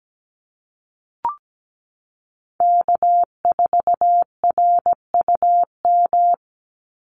Morse code, heard as a keyed tone, about 17 wpm, 700 Hz. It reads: K4RUM
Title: Call Signs Repeated 3x - 17wpm